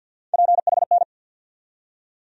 Morse code, heard as a keyed tone, about 49 wpm, 700 Hz. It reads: PHN
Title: USA Call Sign Postfixes - 3 Letters 50wpm